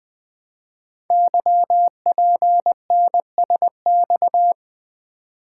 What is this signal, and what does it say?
Morse code, heard as a keyed tone, about 21 wpm, 700 Hz. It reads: YPNSX
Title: Code Groups - 5 Letter:Numbers - 20wpm